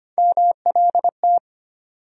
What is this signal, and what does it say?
Morse code, heard as a keyed tone, about 25 wpm, 700 Hz. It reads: MLT
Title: USA Call Sign Postfixes - 3 Letters 25wpm